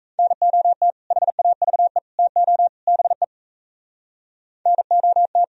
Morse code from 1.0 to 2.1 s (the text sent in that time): HAVE